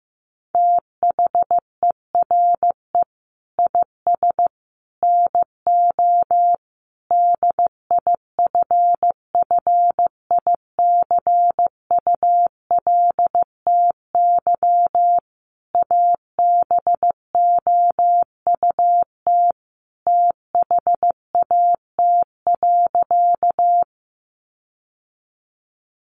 Morse code, heard as a keyed tone, about 15 wpm, 700 Hz. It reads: THERE IS NO DIFFICULTY ABOUT THAT.
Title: Sentences from Top 1500 Words 15wpm